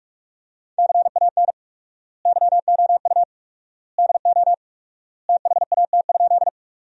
Morse code, heard as a keyed tone, about 41 wpm, 700 Hz. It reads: CAN YOU DO THAT?
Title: 4x Word-Spacing - Sentences from Top 100 Words - 45wpm